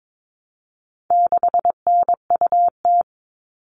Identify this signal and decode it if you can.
Morse code, heard as a keyed tone, about 22 wpm, 700 Hz. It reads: 6NUT
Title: All Cut Numbers - 4-Digit 22wpm